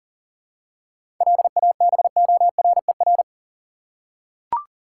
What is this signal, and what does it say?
Morse code, heard as a keyed tone, about 38 wpm, 700 Hz. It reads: LABORER
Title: Practice QSO - Exchange 3 - More Info - 40wpm